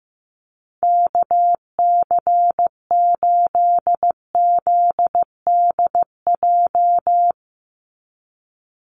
Morse code, heard as a keyed tone, about 15 wpm, 700 Hz. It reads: KC8ZDJ
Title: USA Call Signs 15wpm